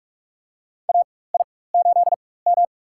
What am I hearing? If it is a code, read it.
Morse code, heard as a keyed tone, about 44 wpm, spaced as for 21 wpm, 700 Hz. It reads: AI8M
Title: USA Call Signs - 4 Characters - 4x Character Spacing - 45wpm